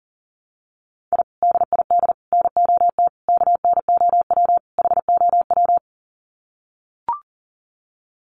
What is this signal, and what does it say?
Morse code, heard as a keyed tone, about 40 wpm, 700 Hz. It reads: I DID NOT KNOW HOW
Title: Sets of 5 Words Encore 40wpm